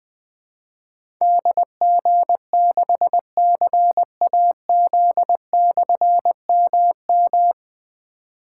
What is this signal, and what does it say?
Morse code, heard as a keyed tone, about 20 wpm, 700 Hz. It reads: DG6CAZ/MM
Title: Difficult Call Signs 20wpm